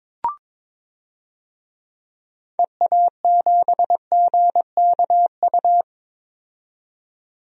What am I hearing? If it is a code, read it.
Morse code, heard as a keyed tone, about 22 wpm, 700 Hz. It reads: EA7GKU
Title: DXCC-100 Call Signs - 6 Characters - 22wpm